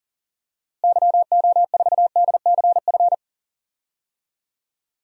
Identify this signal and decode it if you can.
Morse code, heard as a keyed tone, about 37 wpm, 700 Hz. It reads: YO4DCF
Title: DXCC-100 Call Signs - 6 Characters - Repeated 2x - 40wpm